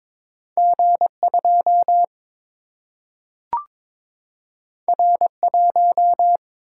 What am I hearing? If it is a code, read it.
Morse code, heard as a keyed tone, about 22 wpm, 700 Hz. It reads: G 2 R 1
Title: Single-Letter Single-Number 22wpm